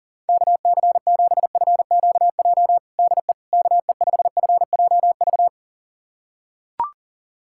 Morse code, heard as a keyed tone, about 40 wpm, 700 Hz. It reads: KC7FQJ DE KE5FJV
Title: USA Callsign DE Callsign - 6 Characters - 40wpm